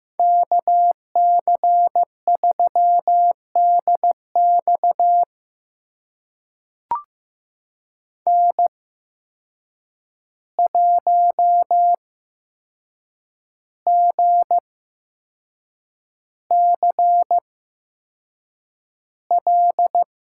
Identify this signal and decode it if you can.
Morse code, heard as a keyed tone, about 15 wpm, spaced as for 6 wpm, 700 Hz. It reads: KC3DX N1GCL
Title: USA Call Signs - 5 Characters - 8x Character Spacing - 15wpm